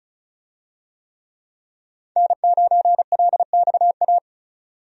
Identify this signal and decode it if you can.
Morse code, heard as a keyed tone, about 36 wpm, 700 Hz. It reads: N9LXA
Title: USA Call Signs - 5 Characters - 35wpm